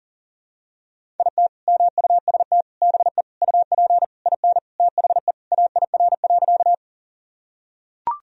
Morse code, heard as a keyed tone, about 40 wpm, 700 Hz. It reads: IT MUST BE UP IN THE AIR.
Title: Sentences from Top 1000 Words - Review Part 1 - 40wpm